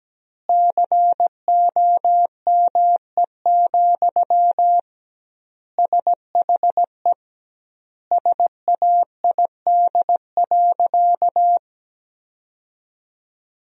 Morse code, heard as a keyed tone, about 17 wpm, 700 Hz. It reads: COME, SHE SAID.
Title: 2x Word-Spacing - Sentences from Top 100 Words - 17wpm